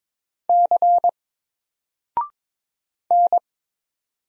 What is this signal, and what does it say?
Morse code, heard as a keyed tone, about 22 wpm, 700 Hz. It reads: C N
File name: Single-Letter-Number-22wpm